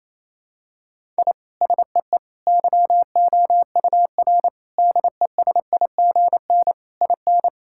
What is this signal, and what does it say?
Morse code, heard as a keyed tone, about 28 wpm, 700 Hz. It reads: I SEE YOUR DESIGN IN
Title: Sentences from Top 1000 Words - Review Part 5 - 28wpm